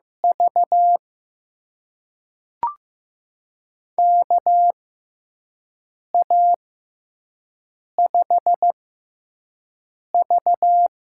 Morse code, heard as a keyed tone, about 15 wpm, spaced as for 5 wpm, 700 Hz. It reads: V KA5V
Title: USA Call Signs - 4 Characters - 6x Character Spacing - 15wpm